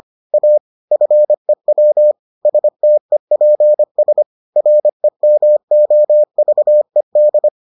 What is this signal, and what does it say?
Morse code, heard as a keyed tone, about 25 wpm, 600 Hz. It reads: A FEW STEPS REMOVED